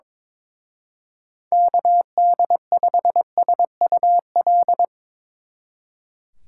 Morse code, heard as a keyed tone, about 24 wpm, 700 Hz. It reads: KD5SUL
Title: Speed Racing - USA 2x3 Call Signs 22wpm